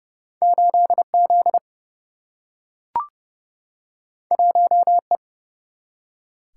Morse code, heard as a keyed tone, about 30 wpm, 700 Hz. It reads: 8 Z 1 E
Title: Single-Letter Single-Number 30wpm